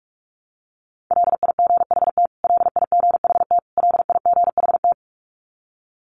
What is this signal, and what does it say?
Morse code, heard as a keyed tone, about 44 wpm, 700 Hz. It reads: LIGHT LIGHT LIGHT
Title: Top 200 Words Repeated 45wpm